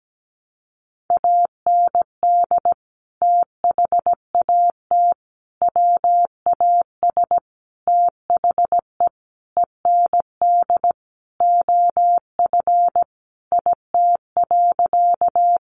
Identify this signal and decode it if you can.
Morse code, heard as a keyed tone, about 17 wpm, 700 Hz. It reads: AND THAT WAS THE END OF IT.
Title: Sentences from Top 200 Words 17wpm